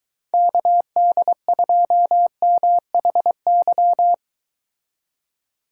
Morse code, heard as a keyed tone, about 22 wpm, 700 Hz. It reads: KD2MHY